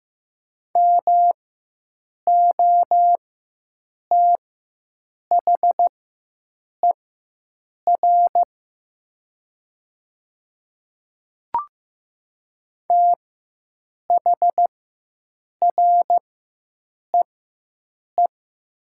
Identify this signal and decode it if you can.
Morse code, heard as a keyed tone, about 15 wpm, spaced as for 7 wpm, 700 Hz. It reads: MOTHER THREE
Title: 4x Character Spacing - Top 300 Words - 15wpm